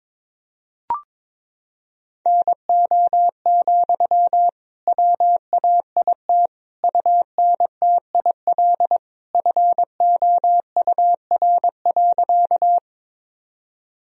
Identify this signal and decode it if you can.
Morse code, heard as a keyed tone, about 22 wpm, 700 Hz. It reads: NO, WAIT UNTIL FOUR.
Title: Sentences from Top 1000 Words - Review Part 1 - 22wpm